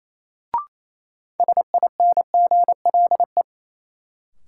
Morse code, heard as a keyed tone, about 28 wpm, 700 Hz. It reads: SINGLE
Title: Sentences from Top 700 Words - New Word Drill - 28wpm